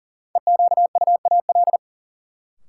Morse code, heard as a keyed tone, about 39 wpm, 700 Hz. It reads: EQUAL
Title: Sentences from Top 1400 Words - New Word Drill - 40wpm